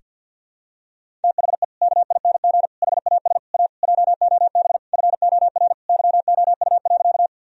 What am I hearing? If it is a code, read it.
Morse code, heard as a keyed tone, about 49 wpm, 700 Hz. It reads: THE KING HAS A JOB FOR YOU.
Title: Sentences from Top 1000 Words - Review Part 3 - 50wpm